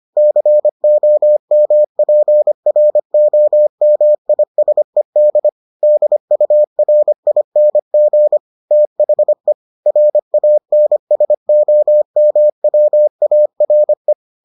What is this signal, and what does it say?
Morse code, heard as a keyed tone, about 25 wpm, 600 Hz. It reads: COMPROMISED DURING THE RANSOMWARE